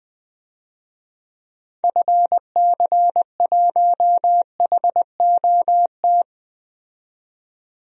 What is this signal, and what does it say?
Morse code, heard as a keyed tone, about 21 wpm, 700 Hz. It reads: FC1HOT
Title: DXCC-100 Call Signs - 6 Characters - 20wpm